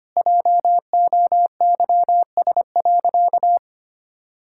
Morse code, heard as a keyed tone, about 25 wpm, 700 Hz. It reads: JOYS.
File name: practice / 25wpm / Difficult Sentences with Hints - 25wpm